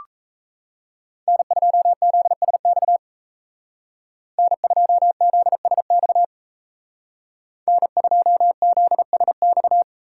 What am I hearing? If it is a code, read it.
Morse code, heard as a keyed tone, about 36 wpm, 700 Hz. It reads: N2ZSX N2ZSX N2ZSX